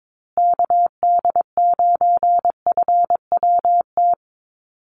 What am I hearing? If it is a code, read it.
Morse code, heard as a keyed tone, about 22 wpm, 700 Hz. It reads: KD9FWT
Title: USA Call Signs Repeated 2x 22wpm